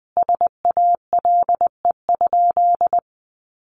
Morse code, heard as a keyed tone, about 20 wpm, 700 Hz. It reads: SALE?
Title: Sentences from Top 1400 Words 20wpm